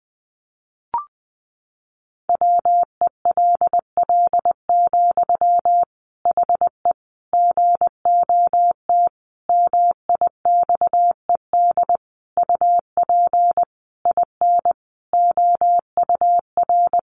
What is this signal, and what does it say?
Morse code, heard as a keyed tone, about 20 wpm, 700 Hz. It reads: WELL, HE GOT MIXED UP IN OUR
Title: Sentences from Top 1500 Words 20wpm